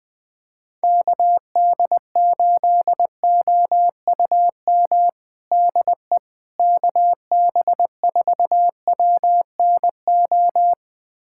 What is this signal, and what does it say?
Morse code, heard as a keyed tone, about 20 wpm, 700 Hz. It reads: KD8OUM DE KB4WNO
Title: USA Callsign DE Callsign 20wpm